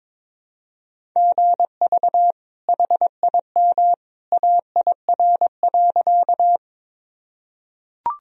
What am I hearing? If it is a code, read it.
Morse code, heard as a keyed tone, about 22 wpm, 700 Hz. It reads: GV HIM AIR.
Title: Sentences from Top 200 Words and Abbreviations 22wpm